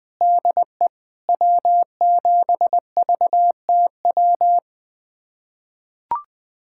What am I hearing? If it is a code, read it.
Morse code, heard as a keyed tone, about 20 wpm, 700 Hz. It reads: DE W7VTW
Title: USA Callsign DE Callsign 20wpm